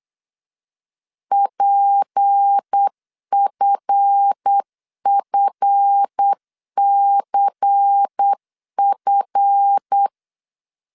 Morse code, heard as a keyed tone, about 8 wpm, 800 Hz. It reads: PFFCF
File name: LEZIONI / 7 GF9Z / 7 GF 8wpm 3108